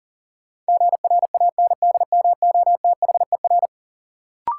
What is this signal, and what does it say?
Morse code, heard as a keyed tone, about 37 wpm, 700 Hz. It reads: GRANDMOTHER
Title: Sentences from Top 1200 Words - New Word Drill - 40wpm